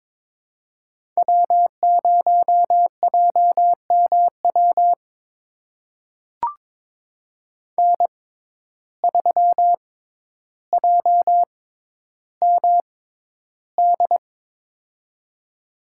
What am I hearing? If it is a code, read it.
Morse code, heard as a keyed tone, about 22 wpm, spaced as for 11 wpm, 700 Hz. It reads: W0JMW N3JMD
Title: USA Call Signs - 5 Characters - 6x Character Spacing - 22wpm